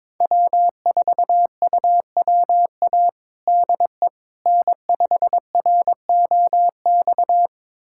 Morse code, heard as a keyed tone, about 22 wpm, 700 Hz. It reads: W4UWA DE N5ROX
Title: USA Callsign DE Callsign - 5 Characters - 22wpm